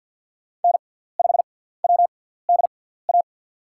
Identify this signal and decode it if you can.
Morse code, heard as a keyed tone, about 53 wpm, spaced as for 17 wpm, 700 Hz. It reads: N5WDA
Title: USA Call Signs - 6x Character Spacing - 50wpm